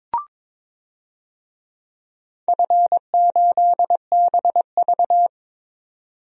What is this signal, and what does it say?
Morse code, heard as a keyed tone, about 22 wpm, 700 Hz. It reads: F8BV
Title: DXCC-100 Call Signs - 4 Characters - 22wpm